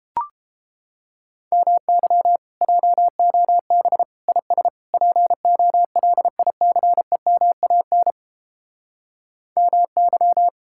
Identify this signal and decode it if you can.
Morse code, heard as a keyed tone, about 29 wpm, 700 Hz. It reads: MY JOB IS POLICEMAN MY